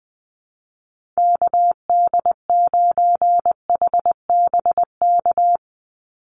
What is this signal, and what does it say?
Morse code, heard as a keyed tone, about 20 wpm, 700 Hz. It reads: KD9HBK
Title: USA Call Signs 20wpm